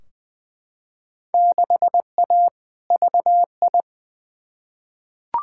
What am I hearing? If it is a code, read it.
Morse code, heard as a keyed tone, about 20 wpm, 700 Hz. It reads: 6A VI
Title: Field Day Exchanges Encore - 20wpm